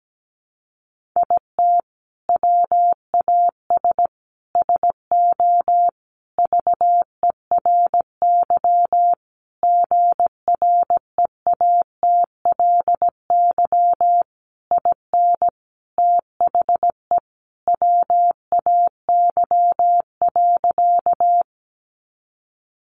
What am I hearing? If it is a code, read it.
Morse code, heard as a keyed tone, about 17 wpm, 700 Hz. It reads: IT WAS SO VERY GREATLY IN THE WAY.
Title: Sentences from Top 1400 Words 17wpm